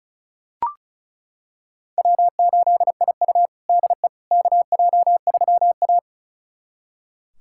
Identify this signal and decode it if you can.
Morse code, heard as a keyed tone, about 34 wpm, 700 Hz. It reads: W8IU DE KJ3A
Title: USA Callsign DE Callsign - 4 Characters - 35wpm